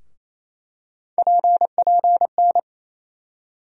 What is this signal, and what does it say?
Morse code, heard as a keyed tone, about 28 wpm, 700 Hz. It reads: PPN
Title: USA Call Sign Postfixes - 3 Letters 28wpm